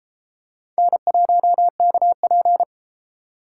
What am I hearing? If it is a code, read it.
Morse code, heard as a keyed tone, about 30 wpm, 700 Hz. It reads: N1KP